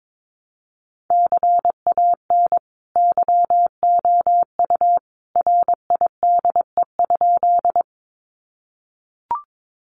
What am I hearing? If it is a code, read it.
Morse code, heard as a keyed tone, about 22 wpm, 700 Hz. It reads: CAN YOU RIDE?
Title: Sentences from Top 1200 Words 22wpm